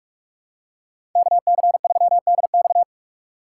Morse code, heard as a keyed tone, about 46 wpm, 700 Hz. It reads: KC3DX
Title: USA Call Signs - 5 Characters - 3x Character Spacing - 45wpm